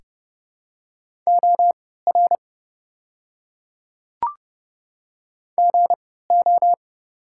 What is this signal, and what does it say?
Morse code, heard as a keyed tone, about 32 wpm, spaced as for 17 wpm, 700 Hz. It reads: OR GO
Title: 3x Character Spacing - Top 100 Words - 30wpm